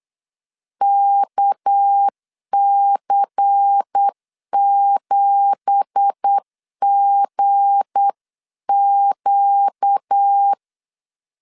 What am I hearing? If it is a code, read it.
Morse code, heard as a keyed tone, about 8 wpm, 800 Hz. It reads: KC7GQ
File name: CW / DET / 1 9 1500C S C 8wpm 5653